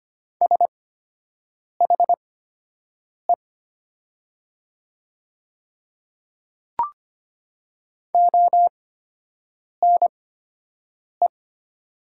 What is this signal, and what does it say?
Morse code, heard as a keyed tone, about 25 wpm, spaced as for 7 wpm, 700 Hz. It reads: SHE ONE
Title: 8x Character Spacing - Top 100 Words - 25wpm